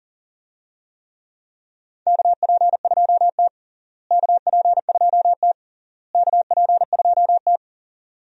Morse code, heard as a keyed tone, about 40 wpm, 700 Hz. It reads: KP2T KP2T KP2T
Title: DXCC-100 Call Signs - 4 Characters - Repeated 3x - 40wpm